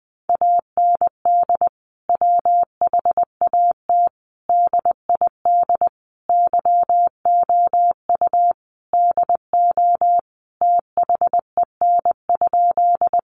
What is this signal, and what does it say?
Morse code, heard as a keyed tone, about 20 wpm, 700 Hz. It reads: AND WHAT DID YOU DO THEN?
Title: Sentences from Top 100 Words 20wpm